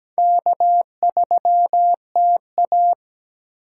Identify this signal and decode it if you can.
Morse code, heard as a keyed tone, about 17 wpm, 700 Hz. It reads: K3TA
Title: USA Call Signs - 4x Character Spacing - 17wpm